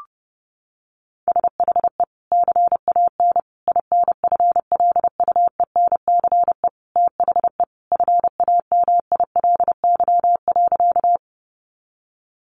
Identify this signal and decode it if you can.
Morse code, heard as a keyed tone, about 30 wpm, 700 Hz. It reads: SHE CAN INFLUENCE THE FAMILY.
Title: Sentences from Top 1100 Words 30wpm